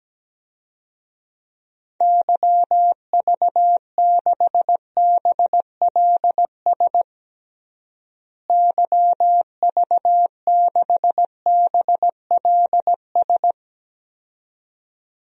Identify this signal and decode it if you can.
Morse code, heard as a keyed tone, about 17 wpm, 700 Hz. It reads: YV6BLS YV6BLS
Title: DXCC-100 Call Signs - 6 Characters - Repeated 2x - 17wpm